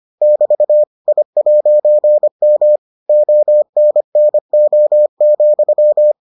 Morse code, heard as a keyed tone, about 25 wpm, 600 Hz. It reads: <BT> I'M ONNO,